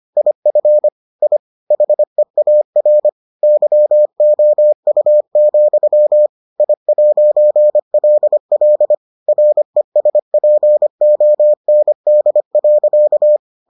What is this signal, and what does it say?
Morse code, heard as a keyed tone, about 25 wpm, 600 Hz. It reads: IF I HEAR YOU, I'LL RESPOND.